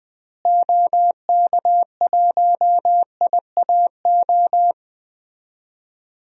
Morse code, heard as a keyed tone, about 20 wpm, 700 Hz. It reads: OK1IAO
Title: DXCC-100 Call Signs - 6 Characters - Repeated 3x - 20wpm